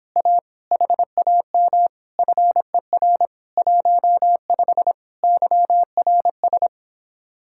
Morse code, heard as a keyed tone, about 29 wpm, 700 Hz. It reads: A HAM FER 15 YRS